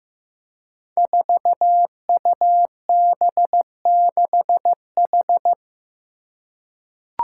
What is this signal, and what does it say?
Morse code, heard as a keyed tone, about 15 wpm, 700 Hz. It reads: 4UB6H
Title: Five Challenging Letter-Number Combinations 15wpm